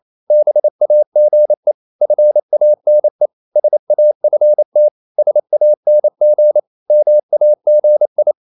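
Morse code, heard as a keyed tone, about 28 wpm, 600 Hz. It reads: BAGE FANE SAFT SANG MAGI
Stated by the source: Danish short words sent in Morse at 28 wpm, from CW med Gnister.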